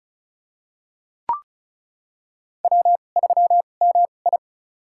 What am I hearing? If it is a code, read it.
Morse code, heard as a keyed tone, about 34 wpm, spaced as for 25 wpm, 700 Hz. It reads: W3MI
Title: USA Call Signs - 4 Characters - 2x Character Spacing - 35wpm